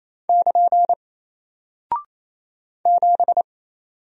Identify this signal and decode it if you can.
Morse code, heard as a keyed tone, about 27 wpm, 700 Hz. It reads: <KN> 7
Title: Single Character (Letter:Number:Punctuation:Prosign) Alt 28wpm